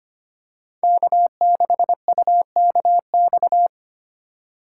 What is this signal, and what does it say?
Morse code, heard as a keyed tone, about 25 wpm, 700 Hz. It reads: K6UKX
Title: USA Call Signs - 5 Characters - 2x Character Spacing - 25wpm